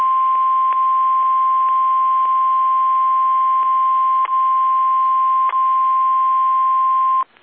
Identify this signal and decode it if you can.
Emergency Alert System (SAME) tones (the 1050 Hz attention tone). It is heard from 0.3 to 6.8 s.